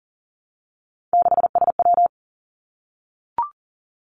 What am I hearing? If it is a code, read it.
Morse code, heard as a keyed tone, about 40 wpm, 700 Hz. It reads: 6SW
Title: Three Challenging Letter-Number Combinations 40wpm